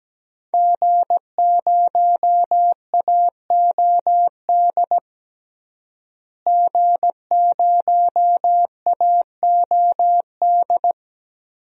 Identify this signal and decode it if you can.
Morse code, heard as a keyed tone, about 17 wpm, 700 Hz. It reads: G0AOD G0AOD
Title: DXCC-100 Call Signs - 5 Characters - Repeated 2x - 17wpm